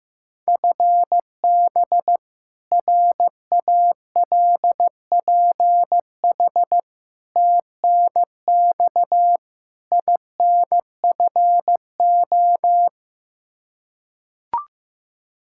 Morse code, heard as a keyed tone, about 16 wpm, 700 Hz. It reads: FB RALPH TNX INFO
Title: Practice QSO - Exchange 3 - More Info - 15wpm